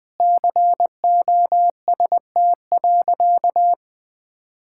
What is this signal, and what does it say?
Morse code, heard as a keyed tone, about 20 wpm, 700 Hz. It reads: COST.
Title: Sentences from Top 1000 Words - Review Part 6 - 20wpm